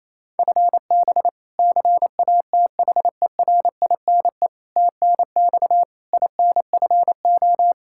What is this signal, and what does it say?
Morse code, heard as a keyed tone, about 28 wpm, 700 Hz. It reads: FB CATHERINE TNX INFO
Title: K1USN SST QSO Practice 28wpm